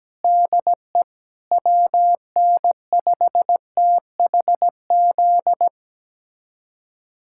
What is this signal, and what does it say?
Morse code, heard as a keyed tone, about 17 wpm, 700 Hz. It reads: DE WN5THZ
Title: USA Callsign DE Callsign - 6 Characters - 17wpm